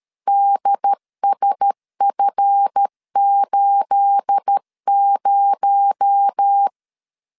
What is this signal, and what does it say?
Morse code, heard as a keyed tone, about 13 wpm, 800 Hz. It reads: DSF80
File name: CW / DET / 1 9 1500C S A 12wpm 3800